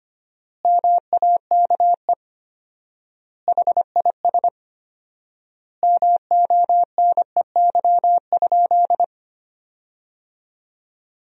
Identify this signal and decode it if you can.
Morse code, heard as a keyed tone, about 25 wpm, 700 Hz. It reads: MAKE HIS MONEY?
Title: 4x Word-Spacing - Sentences from Top 400 Words - 25wpm